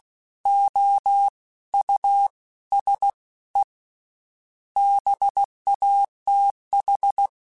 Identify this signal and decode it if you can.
Morse code, heard as a keyed tone, about 16 wpm, 800 Hz. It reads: O U S E BATH
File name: General / General-mp3s / Houses @15